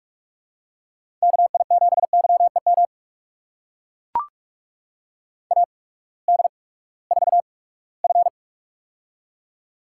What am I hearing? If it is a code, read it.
Morse code, heard as a keyed tone, about 42 wpm, spaced as for 22 wpm, 700 Hz. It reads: KI7YEM AD4F
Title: USA Call Signs - 8x Character Spacing - 45wpm